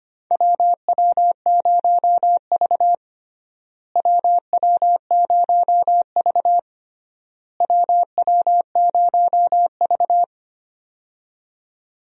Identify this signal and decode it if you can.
Morse code, heard as a keyed tone, about 25 wpm, 700 Hz. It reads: WW0V WW0V WW0V
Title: USA Call Signs - 4 Characters - Repeated 3x - 25wpm